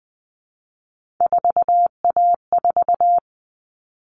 Morse code, heard as a keyed tone, about 20 wpm, 700 Hz. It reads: 4A4
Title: All Cut Numbers - 3-Digit 20wpm